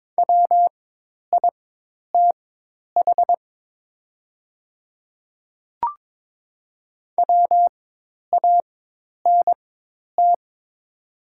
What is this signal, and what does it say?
Morse code, heard as a keyed tone, about 22 wpm, spaced as for 10 wpm, 700 Hz. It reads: WITH WANT
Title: 4x Character Spacing - Top 200 Words - 22wpm